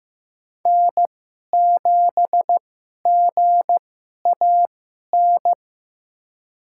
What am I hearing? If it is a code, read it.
Morse code, heard as a keyed tone, about 15 wpm, spaced as for 11 wpm, 700 Hz. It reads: N7GAN
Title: USA Call Signs - 5 Characters - 2x Character Spacing - 15wpm